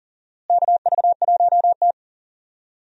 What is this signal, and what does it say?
Morse code, heard as a keyed tone, about 40 wpm, 700 Hz. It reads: KV1T
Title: USA Call Signs - 4 Characters - Repeated 3x - 40wpm